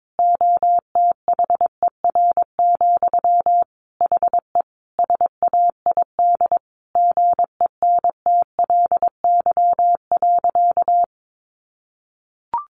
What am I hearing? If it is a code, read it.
Morse code, heard as a keyed tone, about 22 wpm, 700 Hz. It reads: OTHER, HE SAID GENTLY.
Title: Sentences from Top 1500 Words 22wpm